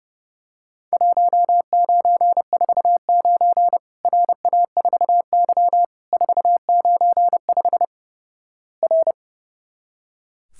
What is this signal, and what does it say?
Morse code, heard as a keyed tone, about 30 wpm, 700 Hz. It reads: 1949 RA4Y 495 R
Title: Worked All Europe DX Contest - QTC Exchange - 30wpm